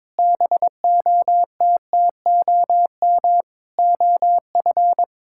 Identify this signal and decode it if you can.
Morse code, heard as a keyed tone, about 22 wpm, 700 Hz. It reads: BOTTOM OF
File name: practice / 22wpm / Sentences with 4-Word Phrases - 22wpm